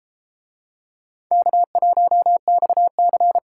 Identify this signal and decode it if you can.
Morse code, heard as a keyed tone, about 30 wpm, 700 Hz. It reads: K1XC